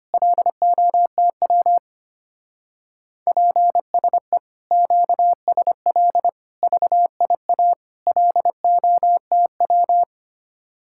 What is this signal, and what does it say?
Morse code, heard as a keyed tone, about 29 wpm, 700 Hz. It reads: LOTW PSE QSL VIA LOTW
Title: Speed-Racing Common QSO Phrases Encore - 25wpm